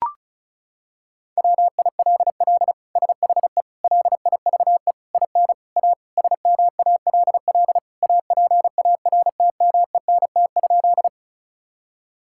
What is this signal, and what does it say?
Morse code, heard as a keyed tone, about 36 wpm, 700 Hz. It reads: WILL SHE LIVE IN A SMALL APARTMENT?
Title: Sentences from Top 1000 Words - Review Part 6 - 35wpm